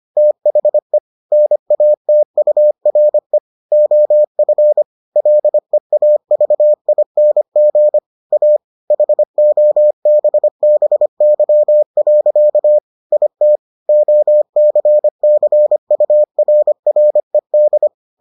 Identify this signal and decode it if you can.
Morse code, heard as a keyed tone, about 25 wpm, 600 Hz. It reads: THE NATURE OF LEAVING A HOBBY. IT OCCURRED